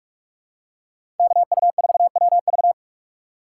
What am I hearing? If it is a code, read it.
Morse code, heard as a keyed tone, about 46 wpm, 700 Hz. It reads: KU4WV
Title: USA Call Signs - 5 Characters - 4x Character Spacing - 45wpm